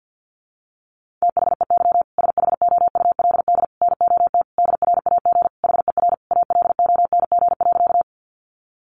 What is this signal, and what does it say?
Morse code, heard as a keyed tone, about 50 wpm, 700 Hz. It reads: THEY SHOULD NOT DRAG HER ALONG.
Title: Sentences from Top 1500 Words 50wpm